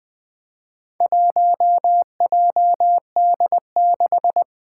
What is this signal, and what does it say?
Morse code, heard as a keyed tone, about 20 wpm, 700 Hz. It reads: 1JD6
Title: Four Challenging Letter-Number Combinations 20wpm